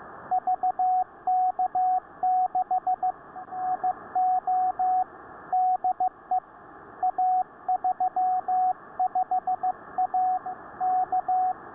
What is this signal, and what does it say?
Morse code, heard as a keyed tone, about 15 wpm, 740 Hz, with radio noise. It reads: VK6 IO DE A35A K